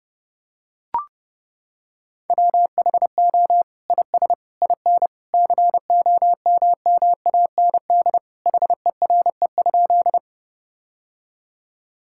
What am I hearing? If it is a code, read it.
Morse code, heard as a keyed tone, about 30 wpm, 700 Hz. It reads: WHO IS IN COMMAND HERE?
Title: Sentences from Top 1600 Words 30wpm